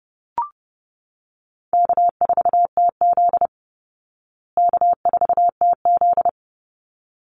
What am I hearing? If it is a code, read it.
Morse code, heard as a keyed tone, about 30 wpm, 700 Hz. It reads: K4TZ K4TZ
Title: Easy Call Signs Repeated 2x - 30wpm